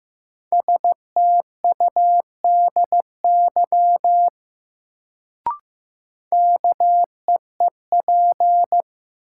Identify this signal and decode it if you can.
Morse code, heard as a keyed tone, about 15 wpm, 700 Hz. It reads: STUDY KEEP
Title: Sentences from Top 300 Words - New Word Drill - 15wpm